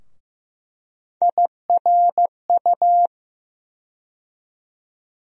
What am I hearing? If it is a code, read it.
Morse code, heard as a keyed tone, about 15 wpm, 700 Hz. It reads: IRU
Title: USA Call Sign Postfixes - 3 Letters 15wpm